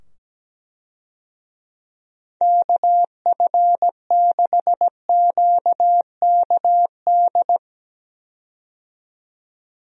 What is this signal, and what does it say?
Morse code, heard as a keyed tone, about 17 wpm, 700 Hz. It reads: KF6QKD
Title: USA Call Signs - 6 Characters - 17wpm